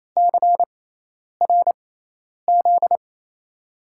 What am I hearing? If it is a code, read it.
Morse code, heard as a keyed tone, about 28 wpm, spaced as for 10 wpm, 700 Hz. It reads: CRZ
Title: USA Call Signs - 6 Characters - 6x Character Spacing - 28wpm